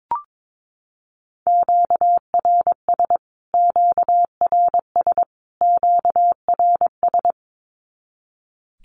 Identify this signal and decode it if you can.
Morse code, heard as a keyed tone, about 22 wpm, 700 Hz. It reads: QRS QRS QRS
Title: Q-Codes Repeated 3x 22wpm